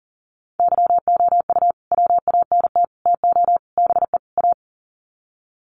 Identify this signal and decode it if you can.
Morse code, heard as a keyed tone, about 40 wpm, 700 Hz. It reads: YOU WANT TO BE A
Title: Sets of 5 Words Encore2 40wpm